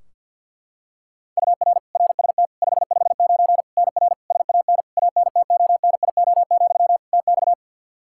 Morse code, heard as a keyed tone, about 53 wpm, 700 Hz. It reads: UR RST 559 NR SAN ANTONIO, TX
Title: K1USN SST QSO Practice 50wpm